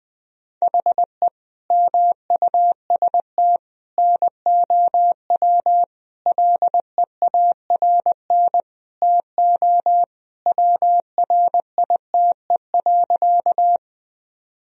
Morse code, heard as a keyed tone, about 21 wpm, 700 Hz. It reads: HE MUST NOW LEARN TO WRITE.
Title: Sentences from Top 1000 Words - Review Part 1 - 20wpm